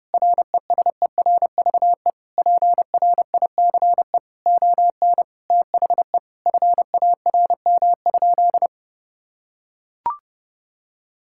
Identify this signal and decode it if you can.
Morse code, heard as a keyed tone, about 32 wpm, 700 Hz. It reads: RESERVE PRICE ON THE FARM?
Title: Sentences from Top 1000 Words - Review Part 6 - 30wpm